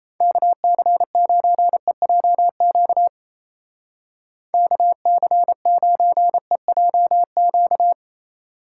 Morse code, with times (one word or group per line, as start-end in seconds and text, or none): 0.2-3.1 KC9EJQ
4.5-7.9 KC9EJQ